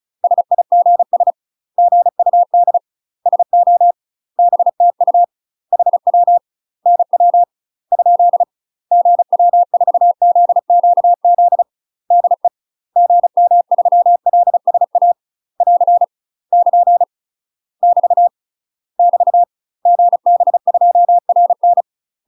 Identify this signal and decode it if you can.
Morse code, heard as a keyed tone, about 34 wpm, 700 Hz. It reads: SIGS GUD SO BTU HW NW ? GW4ZQZ DE GM3LSA <AR> <KN> <BT> <BT> GB2RN
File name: UK MASTER QSOs TRAINING / 35WPM / UK MASTER QSOs 0015